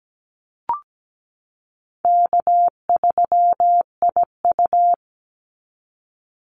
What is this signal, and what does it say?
Morse code, heard as a keyed tone, about 17 wpm, 700 Hz. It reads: K3IU
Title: Easy Call Signs Repeated 3x - 17wpm